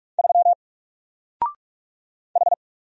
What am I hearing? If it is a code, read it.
Morse code, heard as a keyed tone, about 44 wpm, 700 Hz. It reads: 3 H
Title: Single Letter-Number Alt 45wpm